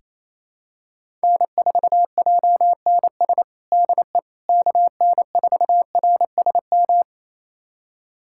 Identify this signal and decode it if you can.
Morse code, heard as a keyed tone, about 28 wpm, 700 Hz. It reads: N4JNS DE KN4RSM
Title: USA Callsign DE Callsign 28wpm